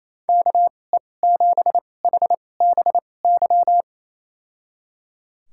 Morse code, heard as a keyed tone, about 28 wpm, spaced as for 20 wpm, 700 Hz. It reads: KE7HBY
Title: USA Call Signs - 6 Characters - 2x Character Spacing - 28wpm